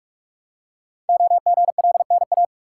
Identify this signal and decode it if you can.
Morse code, heard as a keyed tone, about 44 wpm, 700 Hz. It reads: OGLNA
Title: Code Groups - 5 Letters - 45wpm